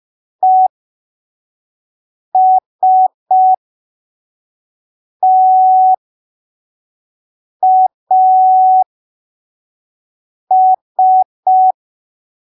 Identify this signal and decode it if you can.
Morse code, heard as a keyed tone, about 5 wpm, spaced as for 3 wpm, 750 Hz. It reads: ESTAS
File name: arquivos / texto / texto 26 05ppm